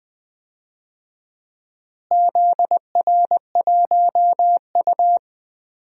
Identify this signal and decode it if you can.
Morse code, heard as a keyed tone, about 20 wpm, 700 Hz. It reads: ZR1U
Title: DXCC-100 Call Signs - 4 Characters - Repeated 2x - 20wpm